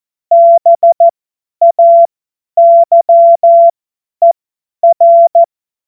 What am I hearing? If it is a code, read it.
Morse code, heard as a keyed tone, about 14 wpm, spaced as for 10 wpm, 680 Hz. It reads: BAYER
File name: Witz-30-September-2025-50-F70